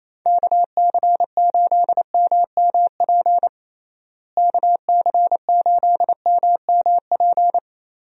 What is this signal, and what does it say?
Morse code, heard as a keyed tone, about 27 wpm, 700 Hz. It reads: KC8MMP KC8MMP
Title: USA Call Signs - 6 Characters - Repeated 3x - 28wpm